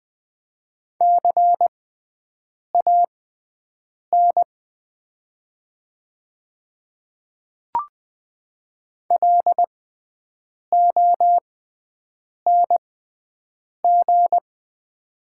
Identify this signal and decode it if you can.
Morse code, heard as a keyed tone, about 20 wpm, spaced as for 7 wpm, 700 Hz. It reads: CAN LONG
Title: 6x Character Spacing - Top 200 Words - 20wpm